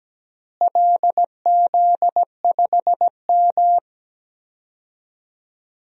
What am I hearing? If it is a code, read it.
Morse code, heard as a keyed tone, about 17 wpm, 700 Hz. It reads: LZ5M
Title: DXCC-100 Call Signs - 4 Characters - Repeated 3x - 17wpm